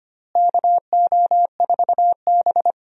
Morse code, heard as a keyed tone, about 25 wpm, 700 Hz. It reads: KO4B
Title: USA Call Signs - 4 Characters - Repeated 3x - 25wpm